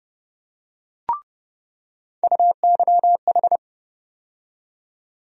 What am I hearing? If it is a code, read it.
Morse code, heard as a keyed tone, about 30 wpm, 700 Hz. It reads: UYH
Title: USA Call Sign Postfixes - 3 Letters 30wpm